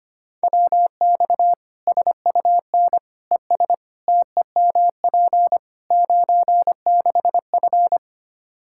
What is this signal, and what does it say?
Morse code, heard as a keyed tone, about 25 wpm, 700 Hz. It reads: WX SUN ES TEMP 96F
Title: Practice QSO - Exchange 2 - Info - 25wpm